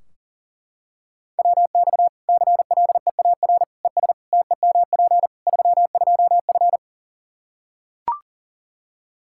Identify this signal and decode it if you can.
Morse code, heard as a keyed tone, about 40 wpm, 700 Hz. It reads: WX CLEAR ES TEMP 32F
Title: Practice QSO - Exchange 2 - Info - 40wpm